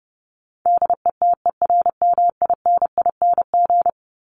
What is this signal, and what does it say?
Morse code, heard as a keyed tone, about 30 wpm, 700 Hz. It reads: DETERMINING
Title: ING Suffix - Top 100 Words with Base Words 30wpm